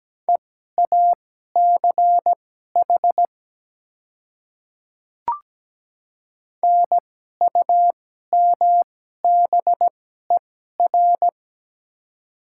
Morse code, heard as a keyed tone, about 17 wpm, spaced as for 12 wpm, 700 Hz. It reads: EACH NUMBER
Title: 2x Character Spacing - Top 200 Words - 17wpm